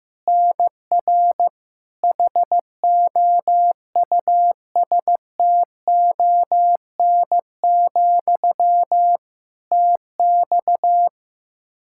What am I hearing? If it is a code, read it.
Morse code, heard as a keyed tone, about 15 wpm, 700 Hz. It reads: NR HOUSTON, TX
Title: K1USN SST QSO Practice 15wpm